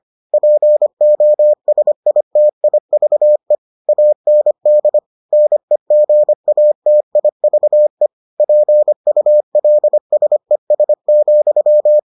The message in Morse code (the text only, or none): POSITIVE AND NEGATIVE PULSES,